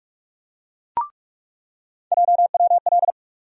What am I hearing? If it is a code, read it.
Morse code, heard as a keyed tone, about 46 wpm, 700 Hz. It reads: JWL
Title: USA Call Sign Postfixes - 3 Letters 45wpm